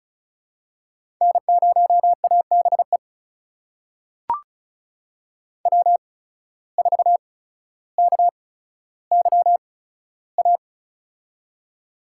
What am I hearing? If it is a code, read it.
Morse code, heard as a keyed tone, about 36 wpm, spaced as for 15 wpm, 700 Hz. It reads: N0ABE W4KYA
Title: USA Call Signs - 5 Characters - 8x Character Spacing - 35wpm